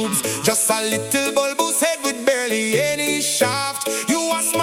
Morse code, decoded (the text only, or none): none